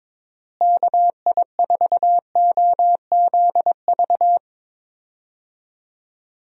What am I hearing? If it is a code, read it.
Morse code, heard as a keyed tone, about 21 wpm, 700 Hz. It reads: KI4OZV
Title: USA Call Signs - 6 Characters - Repeated 3x - 22wpm